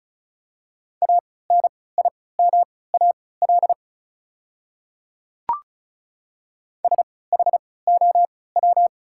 Morse code, heard as a keyed tone, about 34 wpm, spaced as for 20 wpm, 700 Hz. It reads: ANIMAL SHOW
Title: 3x Character Spacing - Top 200 Words - 35wpm